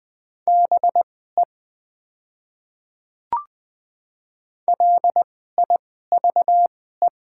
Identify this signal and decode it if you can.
Morse code, heard as a keyed tone, about 20 wpm, spaced as for 14 wpm, 700 Hz. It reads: BE LIVE
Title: 2x Character Spacing - Top 200 Words - 20wpm